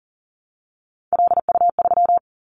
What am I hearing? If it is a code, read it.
Morse code, heard as a keyed tone, about 40 wpm, 700 Hz. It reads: LU3
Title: Callsign Prefixes for Top 50 DXCC - 40wpm